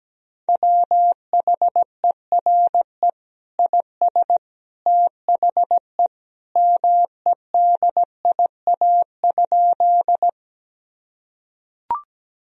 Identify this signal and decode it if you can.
Morse code, heard as a keyed tone, about 17 wpm, 700 Hz. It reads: WHERE IS THE MEDIA?
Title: Sentences from Top 1000 Words - Review Part 4 - 17wpm